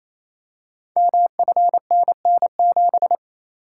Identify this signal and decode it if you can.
Morse code, heard as a keyed tone, about 27 wpm, 700 Hz. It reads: MFNN7
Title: Code Groups - 5 Letter:Numbers - 28wpm